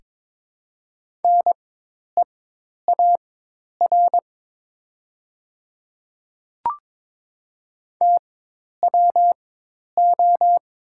Morse code, heard as a keyed tone, about 22 wpm, spaced as for 10 wpm, 700 Hz. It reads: NEAR TWO
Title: 4x Character Spacing - Top 200 Words - 22wpm